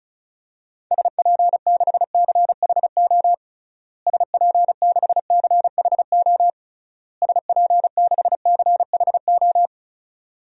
Morse code, heard as a keyed tone, about 34 wpm, 700 Hz. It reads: SP6CHO SP6CHO SP6CHO
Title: DXCC-100 Call Signs - 6 Characters - Repeated 3x - 35wpm